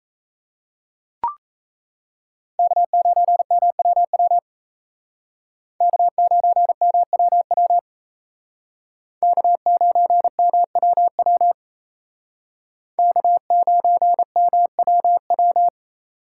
Morse code, heard as a keyed tone, about 35 wpm, 700 Hz. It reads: K9MWW K9MWW K9MWW K9MWW